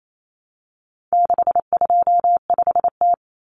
Morse code, heard as a keyed tone, about 28 wpm, 700 Hz. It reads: 625T
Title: Common Cut Numbers - 4-Digit 28wpm